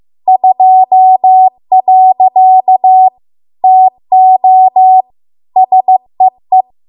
Morse code, heard as a keyed tone, about 15 wpm, 750 Hz. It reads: 2. TO SEE